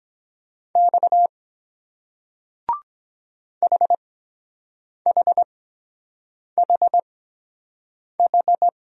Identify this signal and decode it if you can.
Morse code, heard as a keyed tone, about 23 wpm, spaced as for 7 wpm, 700 Hz. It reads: X HHHH